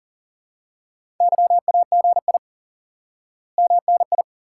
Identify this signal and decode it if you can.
Morse code, heard as a keyed tone, about 40 wpm, 700 Hz. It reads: YAGI MNI
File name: IQE Course 3 - Lesson 89 - YAGI - Two Elements - 40wpm